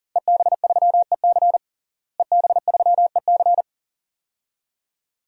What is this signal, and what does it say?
Morse code, heard as a keyed tone, about 40 wpm, 700 Hz. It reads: EB3EC EB3EC
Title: DXCC-100 Call Signs - 5 Characters - Repeated 2x - 40wpm